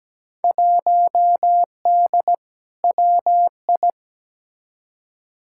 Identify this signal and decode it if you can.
Morse code, heard as a keyed tone, about 17 wpm, 700 Hz. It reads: 1D WI
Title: Field Day Exchanges - 17wpm